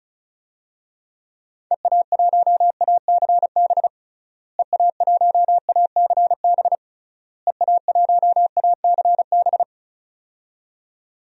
Morse code, heard as a keyed tone, about 36 wpm, 700 Hz. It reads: EA1ACB EA1ACB EA1ACB
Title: DXCC-100 Call Signs - 6 Characters - Repeated 3x - 35wpm